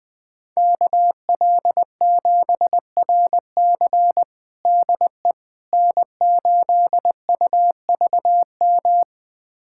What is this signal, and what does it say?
Morse code, heard as a keyed tone, about 21 wpm, 700 Hz. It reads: KL7RC DE N8UVM
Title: USA Callsign DE Callsign - 5 Characters - 20wpm